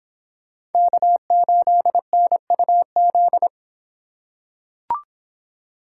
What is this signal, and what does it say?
Morse code, heard as a keyed tone, about 26 wpm, 700 Hz. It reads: K8NUZ